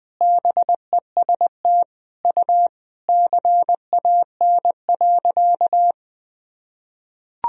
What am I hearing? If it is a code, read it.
Morse code, heard as a keyed tone, about 20 wpm, 700 Hz. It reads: BEST U CAN.
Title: Sentences from Top 500 Words and Abbreviations 20wpm